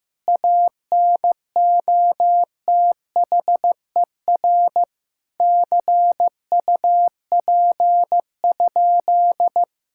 Morse code, heard as a keyed tone, about 16 wpm, 700 Hz. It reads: ANOTHER CUP?
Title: Sentences from Top 1600 Words 15wpm